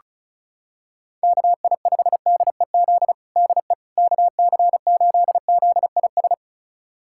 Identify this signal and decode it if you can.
Morse code, heard as a keyed tone, about 36 wpm, 700 Hz. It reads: KI5DEZ DE KC8ZIS
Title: USA Callsign DE Callsign 35wpm